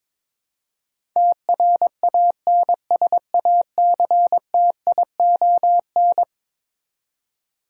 Morse code, heard as a keyed tone, about 22 wpm, 700 Hz. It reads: TRANSACTION
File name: practice / 22wpm / Eleven Letter Words - 22wpm